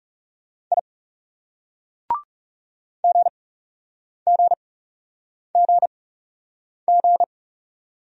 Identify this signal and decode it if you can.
Morse code, heard as a keyed tone, about 40 wpm, spaced as for 8 wpm, 700 Hz. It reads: I GGGG